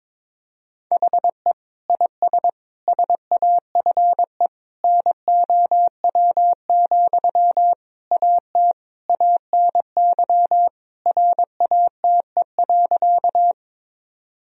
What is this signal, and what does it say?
Morse code, heard as a keyed tone, about 22 wpm, 700 Hz. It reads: HE IS SAFE NOW, AT ANY RATE.
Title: Sentences from Top 1000 Words - Review Part 4 - 22wpm